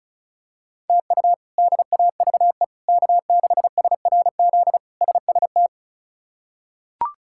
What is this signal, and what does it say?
Morse code, heard as a keyed tone, about 34 wpm, 700 Hz. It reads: TU DAVE K6SRZ SST
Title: K1USN SST QSO Practice - 35wpm